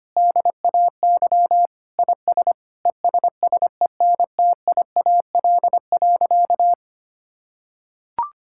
Morse code, heard as a keyed tone, about 25 wpm, 700 Hz. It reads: DAY IS ESSENTIAL.
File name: practice / 25wpm / Difficult Sentences with Hints - 25wpm